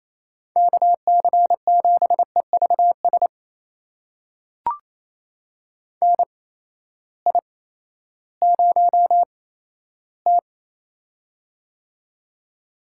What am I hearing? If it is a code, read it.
Morse code, heard as a keyed tone, about 28 wpm, spaced as for 14 wpm, 700 Hz. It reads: KC7EVS NI0T
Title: USA Call Signs - 8x Character Spacing - 28wpm